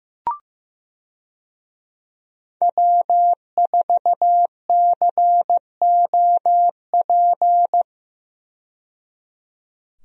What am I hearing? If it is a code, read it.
Morse code, heard as a keyed tone, about 15 wpm, 700 Hz. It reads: W4COP
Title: USA Call Signs - 5 Characters - 15wpm